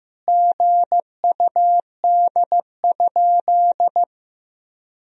Morse code, heard as a keyed tone, about 15 wpm, 700 Hz. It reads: GUD?
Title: Sentences from Top 200 Words and Abbreviations 15wpm